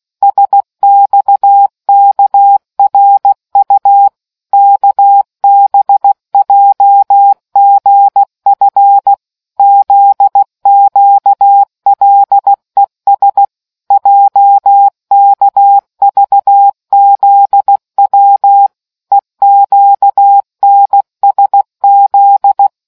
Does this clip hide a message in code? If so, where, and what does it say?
Yes: Morse code from 0.2 to 22.7 s, heard as a keyed tone, about 16 wpm, 800 Hz. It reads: SXKRU KBJGF ZQLES JKVZW EQNSZ